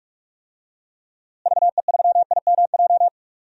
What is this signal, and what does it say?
Morse code, heard as a keyed tone, about 46 wpm, 700 Hz. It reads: VE3IMJ
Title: DXCC-100 Call Signs - 6 Characters - Repeated 2x - 45wpm